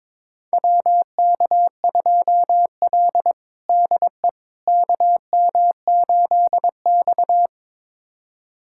Morse code, heard as a keyed tone, about 22 wpm, 700 Hz. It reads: WK2L DE KM8X
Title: USA Callsign DE Callsign - 4 Characters - 22wpm